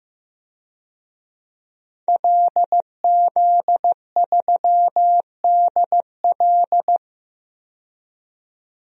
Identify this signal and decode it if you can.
Morse code, heard as a keyed tone, about 16 wpm, 700 Hz. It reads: LZ3DL
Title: DXCC-100 Call Signs - 5 Characters - 15wpm